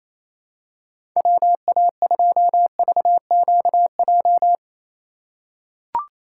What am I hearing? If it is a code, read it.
Morse code, heard as a keyed tone, about 27 wpm, 700 Hz. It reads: WA2VQJ
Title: USA Call Signs - 6 Characters - 4x Character Spacing - 28wpm